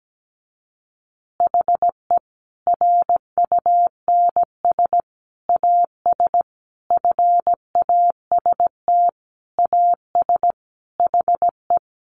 Morse code, heard as a keyed tone, about 17 wpm, 700 Hz. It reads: HE RUNS AS FAST AS HE
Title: Additional Easy Phrases 17wpm